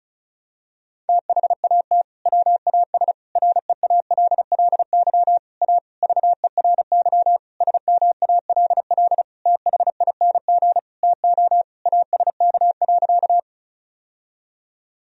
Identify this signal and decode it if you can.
Morse code, heard as a keyed tone, about 35 wpm, 700 Hz. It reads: THAT WAS REALLY A VERY SMALL THING TO ASK.
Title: Sentences from Top 1000 Words - Review Part 1 - 35wpm